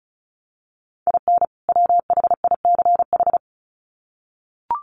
Morse code, heard as a keyed tone, about 34 wpm, 700 Hz. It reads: IN WHICH
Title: Sets of 2 Words 35wpm